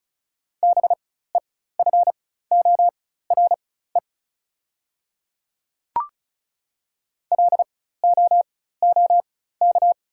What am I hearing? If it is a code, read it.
Morse code, heard as a keyed tone, about 36 wpm, spaced as for 16 wpm, 700 Hz. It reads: BEFORE LOOK
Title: 4x Character Spacing - Top 200 Words - 35wpm